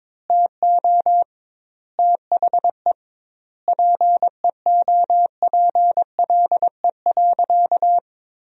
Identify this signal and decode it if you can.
Morse code, heard as a keyed tone, about 22 wpm, 700 Hz. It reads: TO THE PEOPLE.
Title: 2x Word-Spacing - Sentences from Top 400 Words - 22wpm